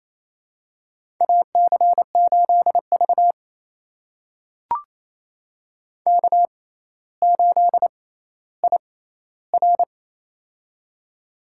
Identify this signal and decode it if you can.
Morse code, heard as a keyed tone, about 28 wpm, spaced as for 14 wpm, 700 Hz. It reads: AC8V K8IR
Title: USA Call Signs - 4 Characters - 6x Character Spacing - 28wpm